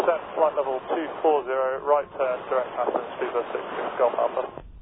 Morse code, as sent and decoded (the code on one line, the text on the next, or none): none